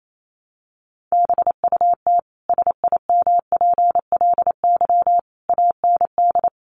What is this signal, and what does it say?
Morse code, heard as a keyed tone, about 28 wpm, 700 Hz. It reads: BUT SIMPLY AND
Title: Sentences from Top 1300 Words 28wpm